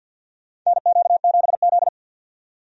Morse code, heard as a keyed tone, about 53 wpm, 700 Hz. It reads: NQ7Z
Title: USA Call Signs - 4 Characters - 50wpm